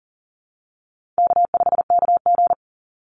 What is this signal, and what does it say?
Morse code, heard as a keyed tone, about 42 wpm, 700 Hz. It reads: K5KG
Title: Call Signs - 40wpm